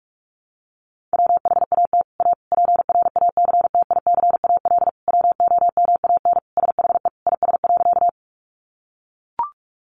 Morse code, heard as a keyed tone, about 44 wpm, 700 Hz. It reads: WHAT A PRACTICAL WOMAN SHE IS.
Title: Sentences from Top 1400 Words 45wpm